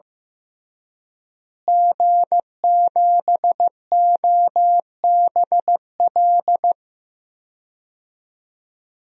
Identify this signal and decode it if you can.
Morse code, heard as a keyed tone, about 15 wpm, 700 Hz. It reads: G7OBL
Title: DXCC-100 Call Signs - 5 Characters - Repeated 3x - 15wpm